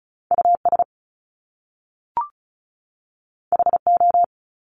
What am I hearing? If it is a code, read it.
Morse code, heard as a keyed tone, about 36 wpm, 700 Hz. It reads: US HO
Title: Top Two Letter Combinations in Words 35wpm